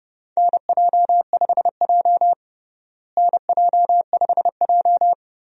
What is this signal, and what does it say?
Morse code, heard as a keyed tone, about 32 wpm, 700 Hz. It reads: NJ5J NJ5J
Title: USA Call Signs - 4 Characters - Repeated 3x - 30wpm